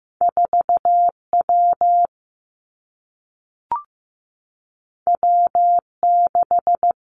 Morse code, heard as a keyed tone, about 15 wpm, 700 Hz. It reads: 4 W W 6
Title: Two Challenging Letter-Number Combinations 15wpm